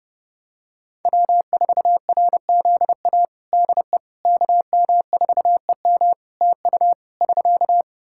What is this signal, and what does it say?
Morse code, heard as a keyed tone, about 30 wpm, 700 Hz. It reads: W4RZA DE KM4EM TU <SK>
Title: Practice QSO - Exchange 4 - Ending - 30wpm